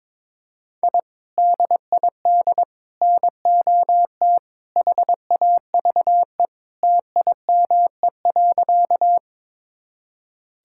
Morse code, heard as a keyed tone, about 22 wpm, 700 Hz. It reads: I DID NOT HAVE TIME.
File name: practice / 22wpm / Sentences with 3-Word Phrases - 22wpm